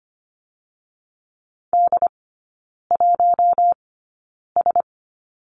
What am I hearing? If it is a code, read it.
Morse code, heard as a keyed tone, about 25 wpm, spaced as for 9 wpm, 700 Hz. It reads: D1S
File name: Lesson 17e - Letter u - Three Characters 25wpm